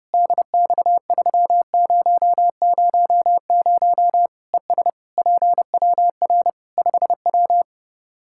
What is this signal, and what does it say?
Morse code, heard as a keyed tone, about 30 wpm, 700 Hz. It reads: DX3000 ES PWR 5W
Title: K1USN SST QSO Practice 30wpm